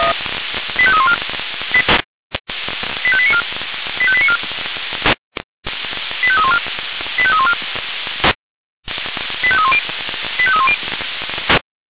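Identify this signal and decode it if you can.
Five-tone selective call (ZVEI), heard as quick runs of five tones, tones of about 70 ms. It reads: 85314 85894 85894 85314 85314 85310 85310
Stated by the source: ZVEI five-tone selective call received on SDR.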